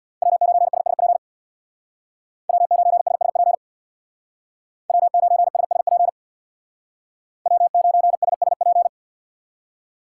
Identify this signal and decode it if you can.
Morse code, heard as a keyed tone, about 64 wpm, 700 Hz. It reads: W9SSP W9SSP W9SSP W9SSP